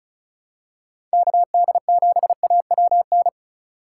Morse code, heard as a keyed tone, about 36 wpm, 700 Hz. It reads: KD7AWN
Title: USA Call Signs - 6 Characters - 4x Character Spacing - 35wpm